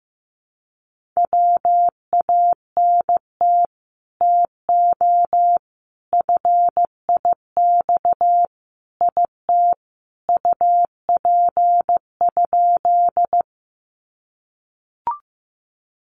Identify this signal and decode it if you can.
Morse code, heard as a keyed tone, about 15 wpm, 700 Hz. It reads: WANT TO FIX IT UP?
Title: Sentences from Top 1400 Words 15wpm